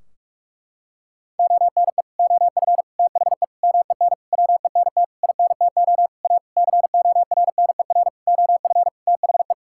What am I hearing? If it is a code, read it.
Morse code, heard as a keyed tone, about 46 wpm, 700 Hz. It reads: ONE OF THE MEN WENT INTO A CORNER OF THE
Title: Sentences from Top 1600 Words 45wpm